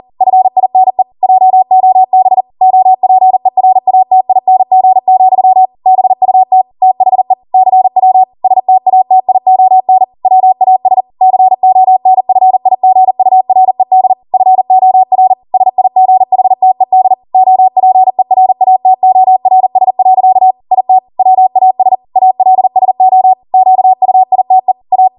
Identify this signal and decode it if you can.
Morse code, heard as a keyed tone, about 40 wpm, 750 Hz. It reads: FINE JOB OPERATING, BUT THE CW STATION WAS CONFIGURED FOR SIGHTED OPERATORS. IT WAS ALSO QUITE A